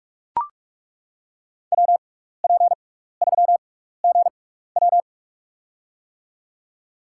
Morse code, heard as a keyed tone, about 41 wpm, spaced as for 15 wpm, 700 Hz. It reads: WP3GW
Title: USA Call Signs - 5 Characters - 6x Character Spacing - 45wpm